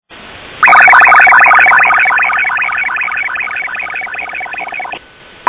SSTV (slow-scan television), PD 50.